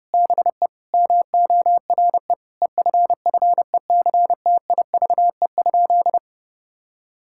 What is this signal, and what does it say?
Morse code, heard as a keyed tone, about 30 wpm, 700 Hz. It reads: BE MORE EFFECTIVE?
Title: Sentences from Top 1000 Words - Review Part 6 - 30wpm